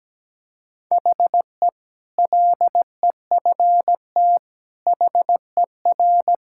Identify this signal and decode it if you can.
Morse code, heard as a keyed tone, about 17 wpm, 700 Hz. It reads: HE LEFT HER
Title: Sentences from Top 1000 Words - Review Part 6 - 17wpm